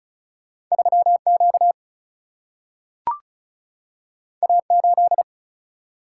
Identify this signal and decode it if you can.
Morse code, heard as a keyed tone, about 36 wpm, 700 Hz. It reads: 3Q A8
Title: Single-Letter Single-Number 35wpm